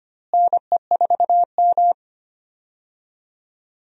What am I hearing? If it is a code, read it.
Morse code, heard as a keyed tone, about 25 wpm, 700 Hz. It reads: NE4M
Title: USA Call Signs - 4 Characters - Repeated 3x - 25wpm